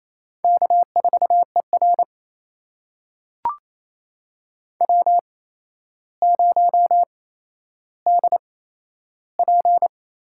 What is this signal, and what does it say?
Morse code, heard as a keyed tone, about 28 wpm, spaced as for 12 wpm, 700 Hz. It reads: K4ER W0DP
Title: USA Call Signs - 8x Character Spacing - 28wpm